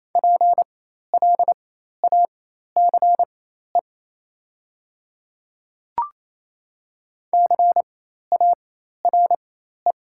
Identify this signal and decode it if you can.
Morse code, heard as a keyed tone, about 28 wpm, spaced as for 13 wpm, 700 Hz. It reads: PLACE CARE
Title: 4x Character Spacing - Top 500 Words - 28wpm